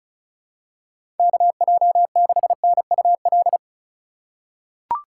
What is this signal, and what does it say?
Morse code, heard as a keyed tone, about 36 wpm, 700 Hz. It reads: KJ6NUL
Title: USA Call Signs - 6 Characters - 4x Character Spacing - 35wpm